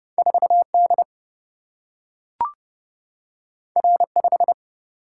Morse code, heard as a keyed tone, about 30 wpm, 700 Hz. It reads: 4 D R 5
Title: Single-Letter Single-Number 30wpm